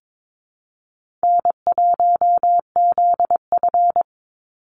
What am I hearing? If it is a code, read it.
Morse code, heard as a keyed tone, about 22 wpm, 700 Hz. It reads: N1ZF
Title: NA SOTA Chaser Callsigns 22wpm